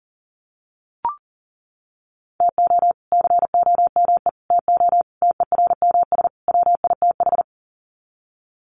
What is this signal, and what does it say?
Morse code, heard as a keyed tone, about 40 wpm, 700 Hz. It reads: TO COME TO TERMS WITH
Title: Sets of 5 Words Encore 40wpm